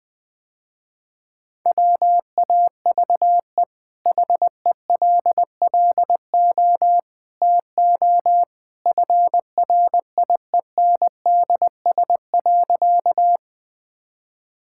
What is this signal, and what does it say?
Morse code, heard as a keyed tone, about 20 wpm, 700 Hz. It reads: WAVE HELLO TO FRIENDS.
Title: Easy Sentences with Hints - 20wpm